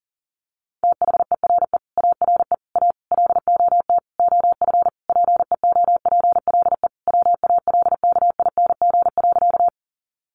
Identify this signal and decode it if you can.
Morse code, heard as a keyed tone, about 40 wpm, 700 Hz. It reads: THERE ARE A LOT OF PEOPLE WALKING.
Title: Easy Phrases 40wpm